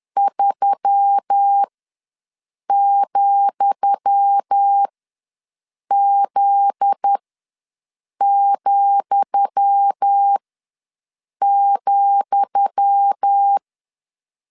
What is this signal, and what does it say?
Morse code, heard as a keyed tone, about 10 wpm, spaced as for 6 wpm, 800 Hz. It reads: 3,Z,,
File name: LEZIONI / 10 SIGNS / 10D COMA ws 10wpm 4314